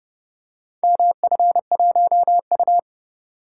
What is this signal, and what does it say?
Morse code, heard as a keyed tone, about 29 wpm, 700 Hz. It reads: MF1U